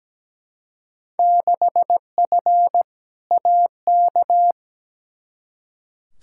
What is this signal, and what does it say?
Morse code, heard as a keyed tone, about 17 wpm, 700 Hz. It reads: 6F AK
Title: Field Day Exchanges Encore - 17wpm